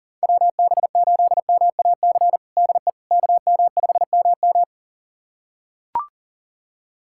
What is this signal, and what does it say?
Morse code, heard as a keyed tone, about 37 wpm, 700 Hz. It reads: WB8MAC DE KM5MM
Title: USA Callsign DE Callsign 40wpm